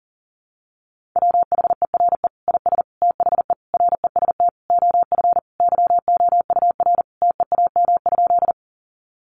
Morse code, heard as a keyed tone, about 40 wpm, 700 Hz. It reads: WHERE IS THE REST OF YOUR TEAM?
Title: Sentences from Top 1500 Words 40wpm